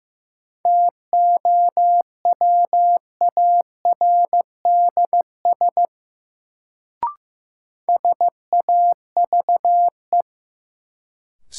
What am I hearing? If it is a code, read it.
Morse code, heard as a keyed tone, about 15 wpm, 700 Hz. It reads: TOWARDS SAVE
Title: Sentences from Top 1100 Words - New Word Drill Alt - 15wpm